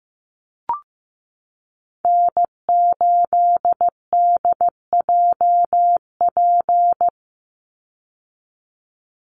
Morse code, heard as a keyed tone, about 15 wpm, 700 Hz. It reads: N8DJP
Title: USA Call Signs 15wpm